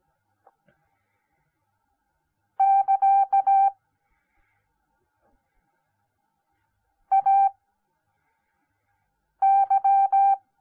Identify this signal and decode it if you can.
Morse code, heard as a keyed tone, about 17 wpm, spaced as for 4 wpm, 780 Hz. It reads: <KA> AY